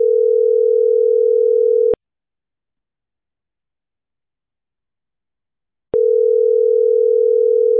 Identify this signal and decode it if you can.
Ringback tone (North America), 440 + 480 Hz.